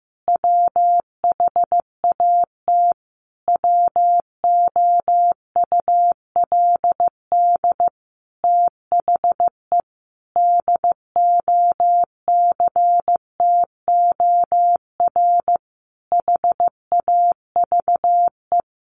Morse code, heard as a keyed tone, about 15 wpm, 700 Hz. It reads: WHAT WOULD THE DOCTOR HAVE
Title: Sentences from Top 1500 Words 15wpm